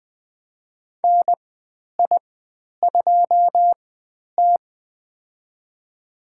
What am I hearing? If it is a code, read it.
Morse code, heard as a keyed tone, about 20 wpm, spaced as for 10 wpm, 700 Hz. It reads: NI2T
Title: 102 Lesson 17g - Letter u - Call Signs 20wpm